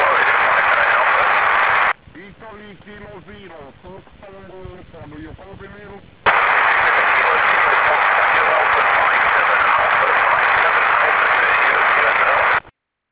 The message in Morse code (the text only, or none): none